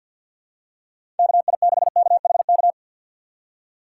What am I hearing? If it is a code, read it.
Morse code, heard as a keyed tone, about 52 wpm, 700 Hz. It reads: KI6KHK
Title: USA Call Signs - 6 Characters - 4x Character Spacing - 50wpm